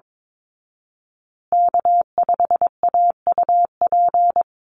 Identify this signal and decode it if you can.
Morse code, heard as a keyed tone, about 22 wpm, 700 Hz. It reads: K5AUP
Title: Call Signs Repeated 3x - 22wpm